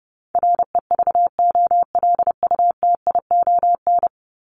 Morse code, heard as a keyed tone, about 30 wpm, 700 Hz. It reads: REVOLUTION
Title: Top 1751-2000 Words 30wpm